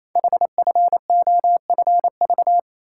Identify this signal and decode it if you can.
Morse code, heard as a keyed tone, about 28 wpm, 700 Hz. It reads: HFOFV
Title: Code Groups - 5 Letters - 28wpm